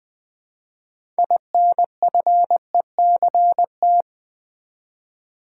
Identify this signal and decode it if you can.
Morse code, heard as a keyed tone, about 20 wpm, 700 Hz. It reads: INFECT
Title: Build A Word - 2 Letters At A Time - 6-Letter Words - 20wpm